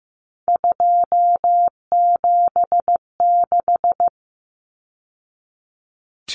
Morse code, heard as a keyed tone, about 15 wpm, 700 Hz. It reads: 276
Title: Numbers - 3-Digit 15wpm